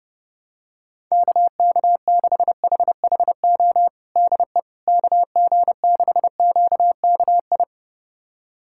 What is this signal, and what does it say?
Morse code, heard as a keyed tone, about 30 wpm, 700 Hz. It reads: KK6HHO DE KG6QKI
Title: USA Callsign DE Callsign 30wpm